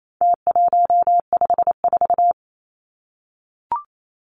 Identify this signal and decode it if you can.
Morse code, heard as a keyed tone, about 28 wpm, 700 Hz. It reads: T154
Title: Numbers - Contest Style - 4-Digit 28wpm